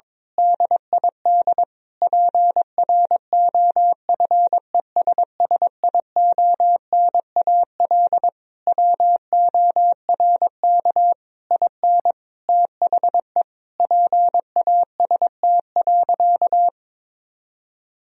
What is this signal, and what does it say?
Morse code, heard as a keyed tone, about 22 wpm, 700 Hz. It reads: DID PROFESSIONAL WORK IN THE PAST.
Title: Sentences from Top 1000 Words - Review Part 4 - 22wpm